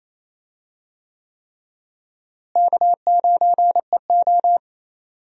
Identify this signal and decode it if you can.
Morse code, heard as a keyed tone, about 28 wpm, 700 Hz. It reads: K9EO
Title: USA Call Signs - 4 Characters - Repeated 3x - 28wpm